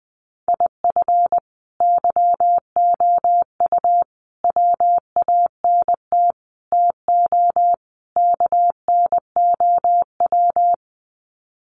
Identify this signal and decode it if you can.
Morse code, heard as a keyed tone, about 20 wpm, 700 Hz. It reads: IF YOU WANT TO KNOW
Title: Sets of 5 Words Encore2 20wpm